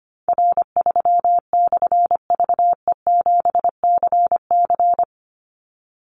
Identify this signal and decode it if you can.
Morse code, heard as a keyed tone, about 25 wpm, 700 Hz. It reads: R3/VE7CC
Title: Call Signs - 25wpm